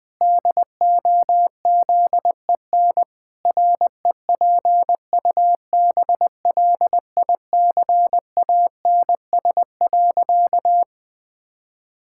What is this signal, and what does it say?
Morse code, heard as a keyed tone, about 21 wpm, 700 Hz. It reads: DOZEN REPUBLICANS.
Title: Pangram Challenge v2 - 20wpm